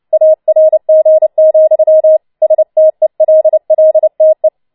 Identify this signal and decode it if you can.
Morse code, heard as a keyed tone, about 29 wpm, 630 Hz. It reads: ARG, STELLTE